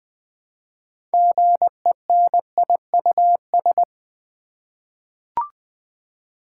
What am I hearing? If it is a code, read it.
Morse code, heard as a keyed tone, about 20 wpm, 700 Hz. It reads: GENIUS
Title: 6 Letter Words - First 3 Letters Given - 20wpm